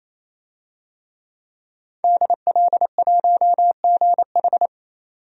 Morse code, heard as a keyed tone, about 28 wpm, 700 Hz. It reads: DL1GH
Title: DXCC-100 Call Signs - 5 Characters - 28wpm